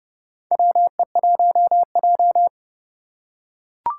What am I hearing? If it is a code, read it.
Morse code, heard as a keyed tone, about 30 wpm, 700 Hz. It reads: WE1J
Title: USA Call Signs - 4 Characters - 4x Character Spacing - 30wpm